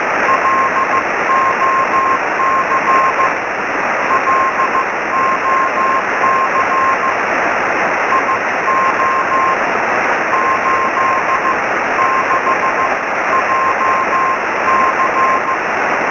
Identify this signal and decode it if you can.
Morse code, heard as a keyed tone, about 15 wpm, spaced as for 12 wpm, 1060 Hz, with noise. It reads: LOC LOC IO86JC
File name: sm7lcb / gm3sbc 23cm